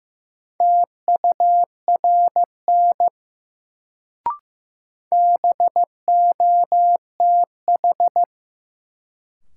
Morse code, heard as a keyed tone, about 15 wpm, 700 Hz. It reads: TURN BOTH
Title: Sets of 4 Words - New Word Drill - 15wpm